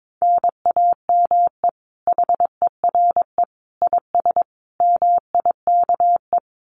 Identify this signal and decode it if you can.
Morse code, heard as a keyed tone, about 22 wpm, 700 Hz. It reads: NAME HERE IS MIKE
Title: Common QSO Phrases 22wpm